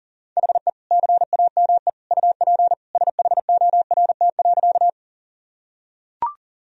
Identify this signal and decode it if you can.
Morse code, heard as a keyed tone, about 40 wpm, 700 Hz. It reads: HE CAME UP SHORT.
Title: Sentences from Top 1000 Words - Review Part 2 - 40wpm